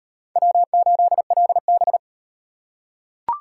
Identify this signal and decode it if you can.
Morse code, heard as a keyed tone, about 36 wpm, 700 Hz. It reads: W8LB